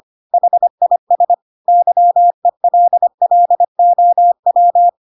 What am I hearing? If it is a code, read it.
Morse code, heard as a keyed tone, about 25 wpm, 700 Hz. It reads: HIS YELLOW
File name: Pinocchio / Pinocchio ch3 25 25 ik1hge